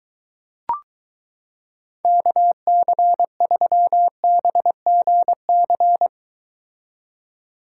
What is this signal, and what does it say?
Morse code, heard as a keyed tone, about 23 wpm, 700 Hz. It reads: KC3BGC